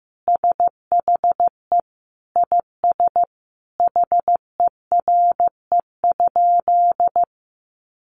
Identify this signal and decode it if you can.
Morse code, heard as a keyed tone, about 15 wpm, 700 Hz. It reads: SHE IS HERE?
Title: Sentences from Top 1300 Words 15wpm